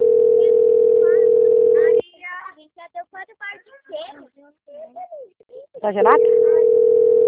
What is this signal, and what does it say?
Ringback tone (North America), 440 + 480 Hz, with speech.
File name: out- - - - - .4516